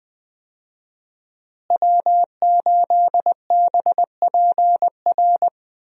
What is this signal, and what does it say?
Morse code, heard as a keyed tone, about 20 wpm, 700 Hz. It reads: W8BPR
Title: USA Call Signs - 5 Characters - Repeated 3x - 20wpm